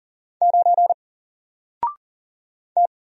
Morse code, heard as a keyed tone, about 40 wpm, 700 Hz. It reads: 9 T
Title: Single Letter-Number 40wpm